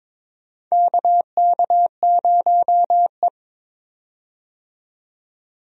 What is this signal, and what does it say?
Morse code, heard as a keyed tone, about 22 wpm, 700 Hz. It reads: KK0E
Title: USA Call Signs - 4 Characters - 22wpm